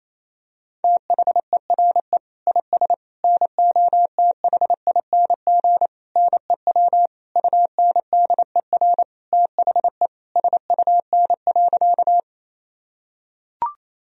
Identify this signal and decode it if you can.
Morse code, heard as a keyed tone, about 28 wpm, 700 Hz. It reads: THERE IS NOTHING NEW UNDER THE SUN.
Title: Sentences from Top 1000 Words - Review Part 1 - 28wpm